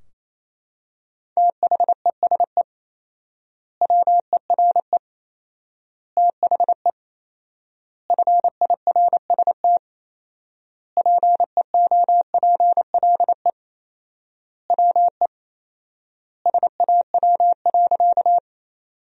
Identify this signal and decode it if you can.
Morse code, heard as a keyed tone, about 28 wpm, 700 Hz. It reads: THESE WERE THE FIRST PEOPLE WE SAW.
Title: 4x Word-Spacing - Sentences from Top 300 Words - 28wpm